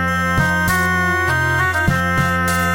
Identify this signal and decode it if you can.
There is no code.